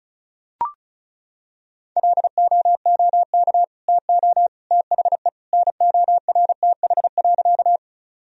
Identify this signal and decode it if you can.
Morse code, heard as a keyed tone, about 36 wpm, 700 Hz. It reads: LOOK TO THE NORTH.
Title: Sentences from Top 1000 Words - Review Part 1 - 35wpm